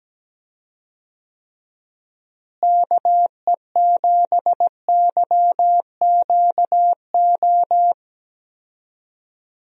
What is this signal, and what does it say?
Morse code, heard as a keyed tone, about 17 wpm, 700 Hz. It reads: KE7YQO
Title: USA Call Signs - 6 Characters - Repeated 2x - 17wpm